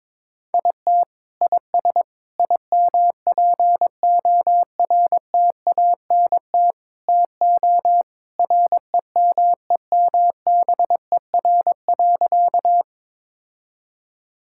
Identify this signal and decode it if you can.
Morse code, heard as a keyed tone, about 22 wpm, 700 Hz. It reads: IT IS IMPORTANT TO REMEMBER.
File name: practice / 22wpm / Sentences with 4-Word Phrases - 22wpm